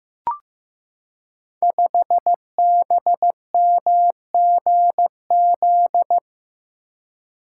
Morse code, heard as a keyed tone, about 15 wpm, 700 Hz. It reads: 5BMGZ
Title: Code Groups - 5 Letter:Numbers - 15wpm